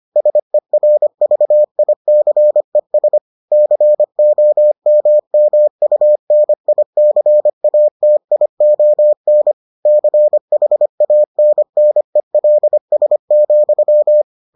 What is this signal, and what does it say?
Morse code, heard as a keyed tone, about 25 wpm, 600 Hz. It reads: SERVICES COMMUNICATION CHANNELS,